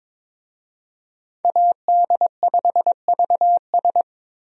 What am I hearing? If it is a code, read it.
Morse code, heard as a keyed tone, about 22 wpm, 700 Hz. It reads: AD5VS
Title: USA Call Signs - 5 Characters - 22wpm